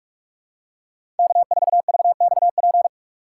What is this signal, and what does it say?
Morse code, heard as a keyed tone, about 47 wpm, 700 Hz. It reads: K4VXP
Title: USA Call Signs - 5 Characters - 6x Character Spacing - 45wpm